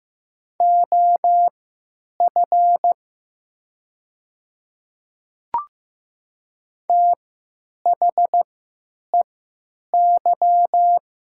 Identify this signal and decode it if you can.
Morse code, heard as a keyed tone, about 15 wpm, spaced as for 9 wpm, 700 Hz. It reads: OF THEY
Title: 3x Character Spacing - Top 100 Words - 15wpm